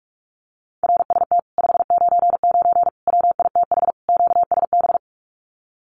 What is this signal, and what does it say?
Morse code, heard as a keyed tone, about 46 wpm, 700 Hz. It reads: RST 599 WITH QSB
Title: Common QSO Phrases Encore 45wpm